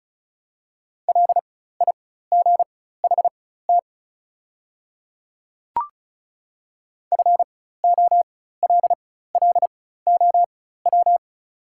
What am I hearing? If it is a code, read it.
Morse code, heard as a keyed tone, about 36 wpm, spaced as for 16 wpm, 700 Hz. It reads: LIGHT FOLLOW
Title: 4x Character Spacing - Top 300 Words - 35wpm